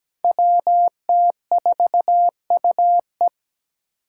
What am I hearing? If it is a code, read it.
Morse code, heard as a keyed tone, about 17 wpm, 700 Hz. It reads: WT4UE
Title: Five Challenging Letter-Number Combinations 17wpm